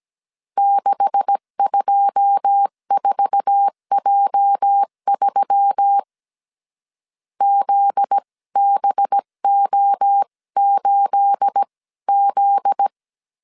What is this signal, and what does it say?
Morse code, heard as a keyed tone, about 17 wpm, spaced as for 15 wpm, 800 Hz. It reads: 624J3 ZBO8Z
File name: 1500C S B 16wpm